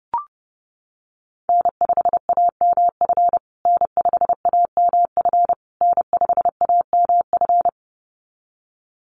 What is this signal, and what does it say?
Morse code, heard as a keyed tone, about 30 wpm, 700 Hz. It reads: N5AMF N5AMF N5AMF
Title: USA Call Signs Repeated 3x 30wpm